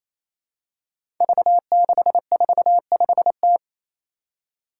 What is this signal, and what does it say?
Morse code, heard as a keyed tone, about 27 wpm, 700 Hz. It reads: V645T
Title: Five Challenging Letter-Number Combinations 28wpm